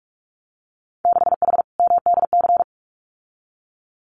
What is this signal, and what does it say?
Morse code, heard as a keyed tone, about 46 wpm, 700 Hz. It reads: 6H MDC
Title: Winter Field Day Exchanges Encore 45wpm